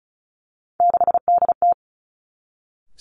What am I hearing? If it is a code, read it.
Morse code, heard as a keyed tone, about 36 wpm, 700 Hz. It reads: 6DT
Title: Three Challenging Letter-Number Combinations 35wpm